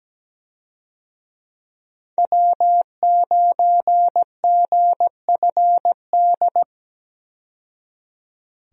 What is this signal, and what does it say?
Morse code, heard as a keyed tone, about 17 wpm, 700 Hz. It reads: W9GFD
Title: USA Call Signs - 5 Characters - 17wpm